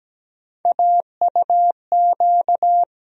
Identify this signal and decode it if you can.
Morse code, heard as a keyed tone, about 17 wpm, 700 Hz. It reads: AUQ
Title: USA Call Sign Postfixes - 3 Letters 17wpm